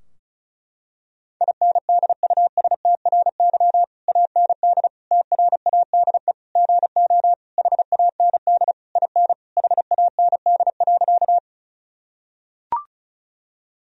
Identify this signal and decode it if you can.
Morse code, heard as a keyed tone, about 36 wpm, 700 Hz. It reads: INDUSTRY AND TRADE GO HAND IN HAND.
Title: Sentences from Top 1000 Words - Review Part 4 - 35wpm